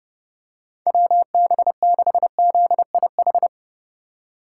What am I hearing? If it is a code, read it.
Morse code, heard as a keyed tone, about 29 wpm, 700 Hz. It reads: WB6ZIH
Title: USA Call Signs - 6 Characters - 4x Character Spacing - 30wpm